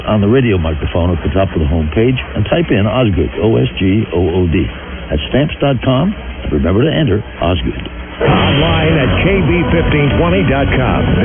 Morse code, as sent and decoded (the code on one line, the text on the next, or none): none